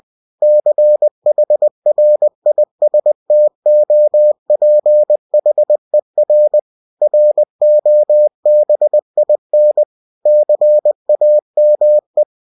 Morse code, heard as a keyed tone, about 20 wpm, 600 Hz. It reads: CHRISTOPHER ROBIN CAME